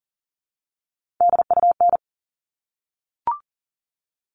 Morse code, heard as a keyed tone, about 40 wpm, 700 Hz. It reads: DUN
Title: All Cut Numbers - 3-Digit 40wpm